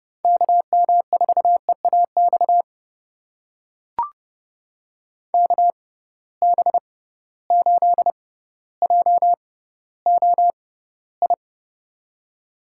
Morse code, heard as a keyed tone, about 30 wpm, spaced as for 15 wpm, 700 Hz. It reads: KM4EAX KB8JOI
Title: USA Call Signs - 6 Characters - 6x Character Spacing - 30wpm